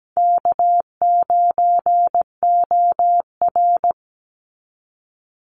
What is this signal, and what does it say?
Morse code, heard as a keyed tone, about 17 wpm, 700 Hz. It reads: K9OR
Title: Call Signs Repeated 2x - 17wpm